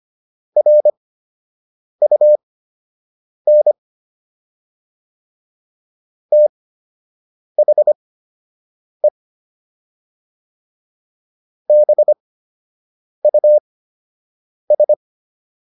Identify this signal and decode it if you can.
Morse code, heard as a keyed tone, about 25 wpm, spaced as for 7 wpm, 600 Hz. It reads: RUN THE BUS